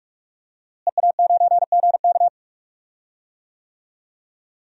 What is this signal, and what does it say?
Morse code, heard as a keyed tone, about 46 wpm, 700 Hz. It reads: EA9GK
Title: DXCC-100 Call Signs - 5 Characters - Repeated 2x - 45wpm